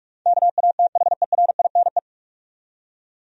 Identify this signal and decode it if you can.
Morse code, heard as a keyed tone, about 46 wpm, 700 Hz. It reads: KATHERINE
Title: USA Common Names - 45wpm